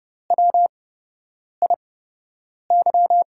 Morse code, heard as a keyed tone, about 30 wpm, spaced as for 8 wpm, 700 Hz. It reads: WIY
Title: USA Call Signs - 5 Characters - 8x Character Spacing - 30wpm